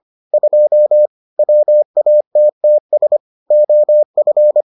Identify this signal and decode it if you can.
Morse code, heard as a keyed tone, about 25 wpm, 600 Hz. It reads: 2 WATTS OF